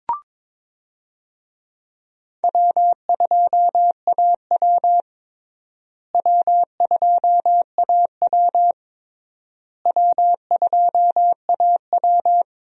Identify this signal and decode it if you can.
Morse code, heard as a keyed tone, about 22 wpm, 700 Hz. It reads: W2AW W2AW W2AW
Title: USA Call Signs - 4 Characters - Repeated 3x - 22wpm